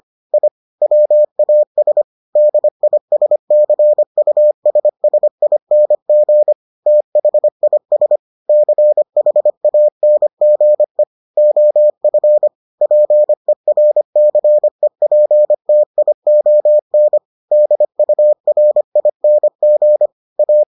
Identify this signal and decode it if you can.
Morse code, heard as a keyed tone, about 25 wpm, 600 Hz. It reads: I WAS DISCUSSING THIS CHANGE OF PERCEPTION DURING A